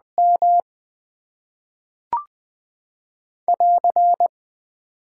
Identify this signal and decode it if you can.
Morse code, heard as a keyed tone, about 20 wpm, 700 Hz. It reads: M <AR>
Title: Top 90 QSO Elements - 6x Character Spacing - 20wpm